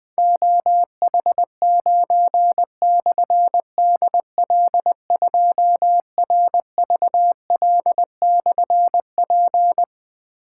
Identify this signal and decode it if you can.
Morse code, heard as a keyed tone, about 20 wpm, 700 Hz. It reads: OH9/DL2RVL/P
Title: Difficult Call Signs 20wpm